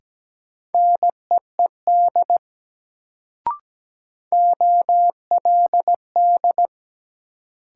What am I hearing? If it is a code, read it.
Morse code, heard as a keyed tone, about 17 wpm, 700 Hz. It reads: NEED OLD
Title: Sets of 2 Words - New Word Drill - 17wpm